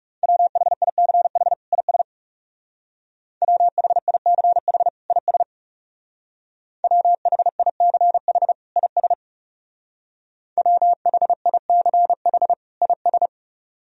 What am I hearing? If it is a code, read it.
Morse code, heard as a keyed tone, about 39 wpm, 700 Hz. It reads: WHICH IS WHICH IS WHICH IS WHICH IS